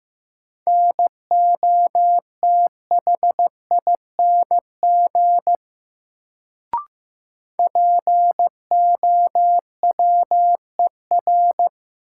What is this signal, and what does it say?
Morse code, heard as a keyed tone, about 15 wpm, 700 Hz. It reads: NOTHING POWER
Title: Sets of 4 Words - New Word Drill - 15wpm